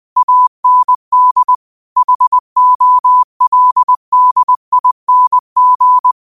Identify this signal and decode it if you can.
Morse code, heard as a keyed tone, about 20 wpm, 1000 Hz. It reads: AND HOLDING
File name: qotdfiles / 20wpm